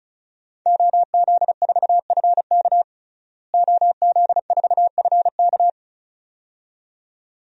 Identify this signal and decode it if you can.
Morse code, heard as a keyed tone, about 34 wpm, 700 Hz. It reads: OZ4FK OZ4FK
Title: DXCC-100 Call Signs - 5 Characters - Repeated 2x - 35wpm